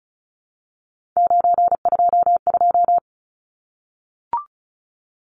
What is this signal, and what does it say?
Morse code, heard as a keyed tone, about 34 wpm, 700 Hz. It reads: 922
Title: Numbers - 3-Digit 35wpm